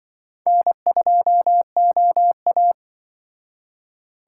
Morse code, heard as a keyed tone, about 24 wpm, 700 Hz. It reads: N2OA
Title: Speed Racing - USA 1x2 Call Signs 20wpm